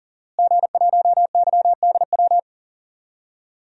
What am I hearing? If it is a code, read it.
Morse code, heard as a keyed tone, about 40 wpm, 700 Hz. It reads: G1YDW
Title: DXCC-100 Call Signs - 5 Characters - Repeated 2x - 40wpm